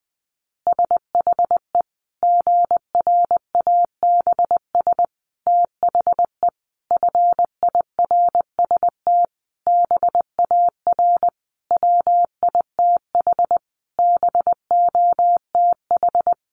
Morse code, heard as a keyed tone, about 20 wpm, 700 Hz. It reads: SHE GRABS THE FIRST BAR WITH BOTH
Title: Easy Phrases 20wpm